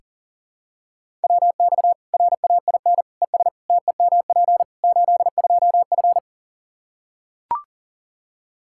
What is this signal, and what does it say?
Morse code, heard as a keyed tone, about 40 wpm, 700 Hz. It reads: WX RAIN ES TEMP 82F
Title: Practice QSO - Exchange 2 - Info - 40wpm